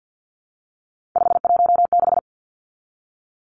Morse code, heard as a keyed tone, about 50 wpm, 700 Hz. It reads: 516
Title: Three Challenging Letter-Number Combinations 50wpm